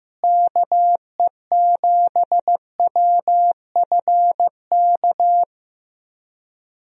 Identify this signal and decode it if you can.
Morse code, heard as a keyed tone, about 15 wpm, 700 Hz. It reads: KE7WFK
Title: USA Call Signs - 6 Characters - Repeated 2x - 15wpm